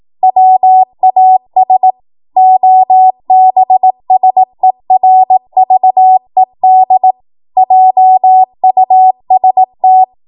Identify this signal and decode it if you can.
Morse code, heard as a keyed tone, about 18 wpm, 750 Hz. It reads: WAS OBSERVED JUST
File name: Morse / 18WPM